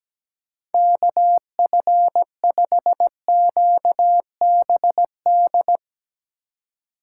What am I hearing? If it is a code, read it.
Morse code, heard as a keyed tone, about 17 wpm, 700 Hz. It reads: KF5QBD
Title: USA Call Signs - 3x Character Spacing - 17wpm